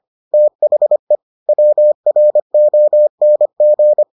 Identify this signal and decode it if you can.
Morse code, heard as a keyed tone, about 25 wpm, 600 Hz. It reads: THE WRONG